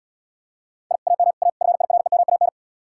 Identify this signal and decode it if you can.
Morse code, heard as a keyed tone, about 80 wpm, 700 Hz. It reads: I AM A FIREMAN